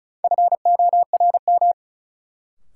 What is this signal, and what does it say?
Morse code, heard as a keyed tone, about 36 wpm, 700 Hz. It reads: FORM
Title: Sentences from Top 1000 Words - Review Part 1 - New Word Drill - 35wpm